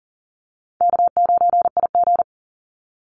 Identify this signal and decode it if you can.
Morse code, heard as a keyed tone, about 40 wpm, 700 Hz. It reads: K9IG
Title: Easy Call Signs - 40wpm